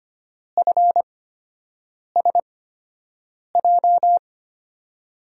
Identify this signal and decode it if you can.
Morse code, heard as a keyed tone, about 25 wpm, spaced as for 7 wpm, 700 Hz. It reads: FSJ
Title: USA Call Signs - 6 Characters - 8x Character Spacing - 25wpm